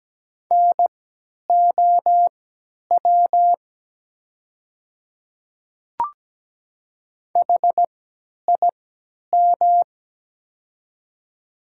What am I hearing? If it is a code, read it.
Morse code, heard as a keyed tone, about 17 wpm, spaced as for 10 wpm, 700 Hz. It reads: NOW HIM
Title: 3x Character Spacing - Top 100 Words - 17wpm